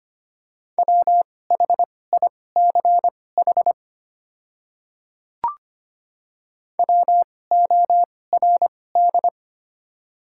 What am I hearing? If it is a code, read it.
Morse code, heard as a keyed tone, about 25 wpm, spaced as for 18 wpm, 690 Hz. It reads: WHICH WORD
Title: 2x Character Spacing - Top 100 Words - 25wpm